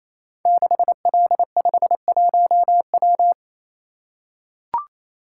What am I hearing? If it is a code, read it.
Morse code, heard as a keyed tone, about 27 wpm, 700 Hz. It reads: 6L51W
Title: Code Groups - 5 Letter:Numbers - 28wpm